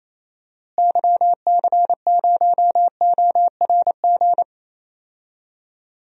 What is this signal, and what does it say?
Morse code, heard as a keyed tone, about 28 wpm, 700 Hz. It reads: YC0ORG
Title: DXCC-100 Call Signs - 6 Characters - Repeated 2x - 28wpm